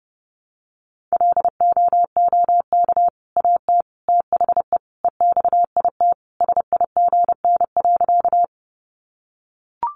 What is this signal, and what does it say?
Morse code, heard as a keyed tone, about 30 wpm, 700 Hz. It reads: LOOK AT THE EXIT SIGN.
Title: Sentences from Top 1100 Words 30wpm